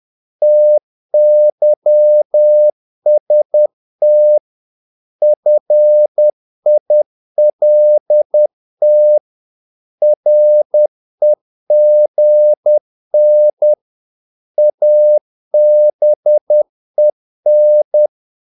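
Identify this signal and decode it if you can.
Morse code, heard as a keyed tone, about 10 wpm, 600 Hz. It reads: TYST FILT REGN ABEN
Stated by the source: Morse practice of short Danish words at 10 wpm, from CW med Gnister.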